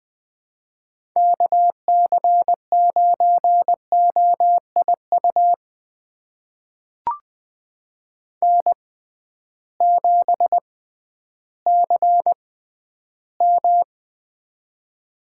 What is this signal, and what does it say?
Morse code, heard as a keyed tone, about 20 wpm, spaced as for 12 wpm, 700 Hz. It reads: KC9OIU N7CM
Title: USA Call Signs - 6x Character Spacing - 20wpm